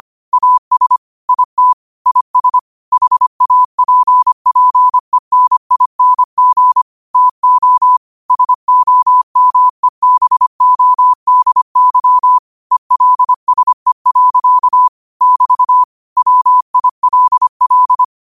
Morse code, heard as a keyed tone, about 25 wpm, 1000 Hz. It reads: AS IT IS HAPPENING TO SOMEBODY ELSE. <BT> WILL
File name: qotdfiles / 25wpm